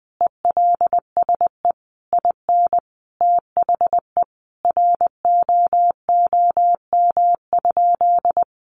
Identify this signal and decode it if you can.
Morse code, heard as a keyed tone, about 20 wpm, 700 Hz. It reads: ELSE IN THE ROOM?
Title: Sentences from Top 900 Words 20wpm